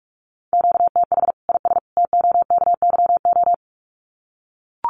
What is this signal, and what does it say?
Morse code, heard as a keyed tone, about 44 wpm, 700 Hz. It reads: QTH IS TOKYO
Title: Common QSO Phrases 45wpm